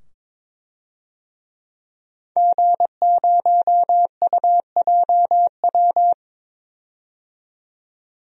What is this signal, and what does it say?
Morse code, heard as a keyed tone, about 21 wpm, 700 Hz. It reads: G0UJW
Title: DXCC-100 Call Signs - 5 Characters - 22wpm